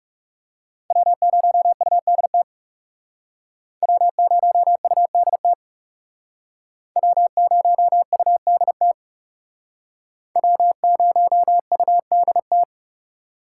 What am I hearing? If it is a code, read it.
Morse code, heard as a keyed tone, about 40 wpm, 700 Hz. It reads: W0UDT W0UDT W0UDT W0UDT